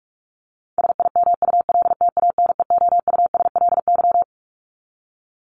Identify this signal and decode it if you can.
Morse code, heard as a keyed tone, about 44 wpm, 700 Hz. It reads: SIMULTANEOUSLY
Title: LY Suffix - Top 100 Words with Base Words 45wpm